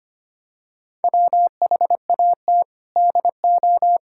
Morse code, heard as a keyed tone, about 25 wpm, 700 Hz. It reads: WHAT DO
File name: practice / 25wpm / Sentences with 3-Word Phrases - 25wpm